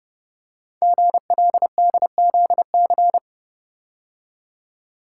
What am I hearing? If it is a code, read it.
Morse code, heard as a keyed tone, about 30 wpm, 700 Hz. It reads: GLDZC
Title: Code Groups - 5 Letters - 30wpm